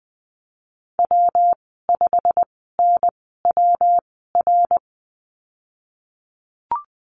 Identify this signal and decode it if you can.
Morse code, heard as a keyed tone, about 20 wpm, spaced as for 15 wpm, 700 Hz. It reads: W5NWR
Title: 193 Lesson 31g - Letter v - Call Signs 20wpm